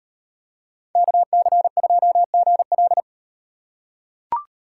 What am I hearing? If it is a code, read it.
Morse code, heard as a keyed tone, about 38 wpm, 700 Hz. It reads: KC2GL